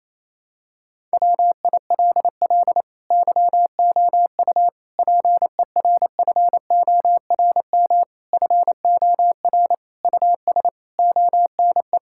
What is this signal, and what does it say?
Morse code, heard as a keyed tone, about 27 wpm, 700 Hz. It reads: WILL YOU PERFORM FOR US ONE
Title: Sentences from Top 1000 Words - Review Part 5 - 28wpm